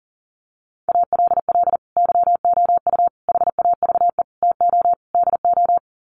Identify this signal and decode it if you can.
Morse code, heard as a keyed tone, about 40 wpm, 700 Hz. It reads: ALL YOU HAVE TO DO
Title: Sets of 5 Words 40wpm